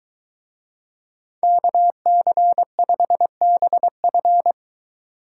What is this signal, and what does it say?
Morse code, heard as a keyed tone, about 23 wpm, 700 Hz. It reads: KC5BF